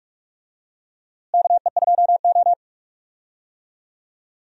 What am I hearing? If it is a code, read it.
Morse code, heard as a keyed tone, about 43 wpm, 700 Hz. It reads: KE2O
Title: USA Call Signs - 4 Characters - Repeated 2x - 45wpm